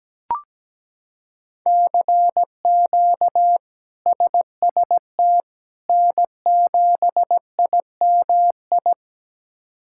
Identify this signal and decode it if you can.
Morse code, heard as a keyed tone, about 17 wpm, 700 Hz. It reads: CQ SST N7IMI
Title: K1USN SST QSO Practice - 17wpm